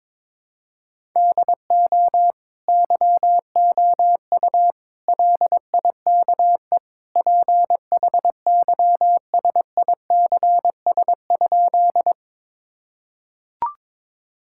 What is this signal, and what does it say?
Morse code, heard as a keyed tone, about 22 wpm, 700 Hz. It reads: DO YOU LIKE PHYSICS?
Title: Sentences from Top 1000 Words - Review Part 5 - 22wpm